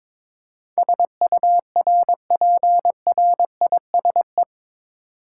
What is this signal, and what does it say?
Morse code, heard as a keyed tone, about 22 wpm, 700 Hz. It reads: SURPRISE
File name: practice / 22wpm / Eight Letter Words - 22wpm